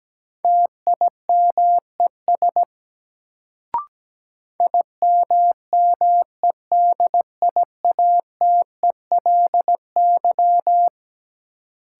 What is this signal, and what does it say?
Morse code, heard as a keyed tone, about 17 wpm, 700 Hz. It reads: TIMES IMMEDIATELY
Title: Sentences from Top 1100 Words - New Word Drill Alt - 17wpm